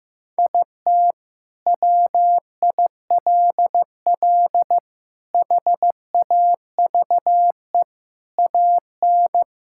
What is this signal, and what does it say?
Morse code, heard as a keyed tone, about 15 wpm, 700 Hz. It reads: IT WILL HAVE AN
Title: Sentences from Top 1000 Words - Review Part 6 - 15wpm